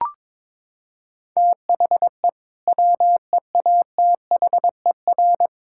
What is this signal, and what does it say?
Morse code, heard as a keyed tone, about 22 wpm, 700 Hz. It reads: THE WEATHER
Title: Sentences from Top 1000 Words - Review Part 5 - 22wpm